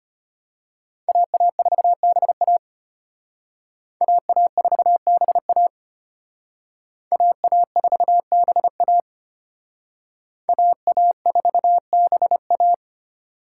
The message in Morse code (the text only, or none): AA4BA AA4BA AA4BA AA4BA